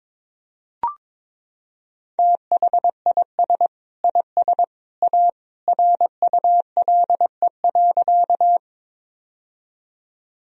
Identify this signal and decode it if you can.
Morse code, heard as a keyed tone, about 22 wpm, 700 Hz. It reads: THIS IS A RULE.
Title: Sentences from Top 1000 Words - Review Part 2 - 22wpm